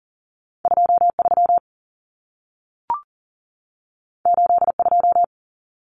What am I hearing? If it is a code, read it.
Morse code, heard as a keyed tone, about 40 wpm, 700 Hz. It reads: 23 82
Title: Numbers - 2-Digit 40wpm